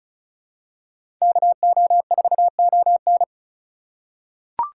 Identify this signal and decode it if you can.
Morse code, heard as a keyed tone, about 36 wpm, 700 Hz. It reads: KO4ON
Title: USA Call Signs - 5 Characters - 4x Character Spacing - 35wpm